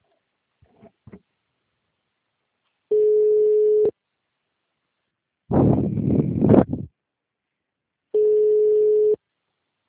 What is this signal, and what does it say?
Ringback tone (Europe, 425 Hz).
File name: out- -2007- - - .110